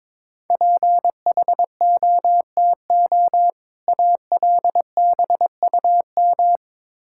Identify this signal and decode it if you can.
Morse code, heard as a keyed tone, about 22 wpm, 690 Hz. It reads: PHOTO ALBUM
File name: Difficult Words with Hints - 22wpm